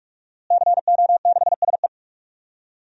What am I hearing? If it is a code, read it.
Morse code, heard as a keyed tone, about 46 wpm, 700 Hz. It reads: CO6SE
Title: DXCC-100 Call Signs - 5 Characters - Repeated 2x - 45wpm